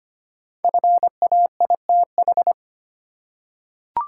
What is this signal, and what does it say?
Morse code, heard as a keyed tone, about 25 wpm, 700 Hz. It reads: FAITH
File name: 5 Letter Words - First 3 Letters Given - 25wpm